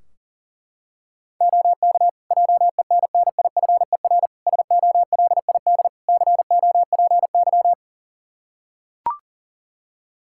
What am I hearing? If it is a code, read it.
Morse code, heard as a keyed tone, about 40 wpm, 700 Hz. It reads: OK JENNIFER SOLID COPY
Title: Practice QSO - Exchange 3 - More Info - 40wpm